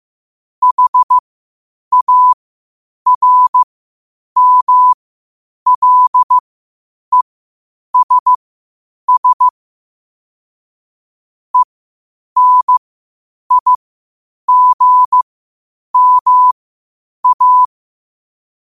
Morse code, heard as a keyed tone, about 15 wpm, spaced as for 8 wpm, 1000 Hz. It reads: HARMLESS ENIGMA
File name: qotdfiles / 8wpm